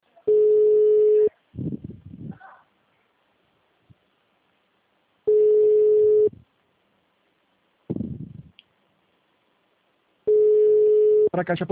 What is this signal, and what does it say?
Ringback tone (Europe, 425 Hz), with speech.
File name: out- -2014- - - .347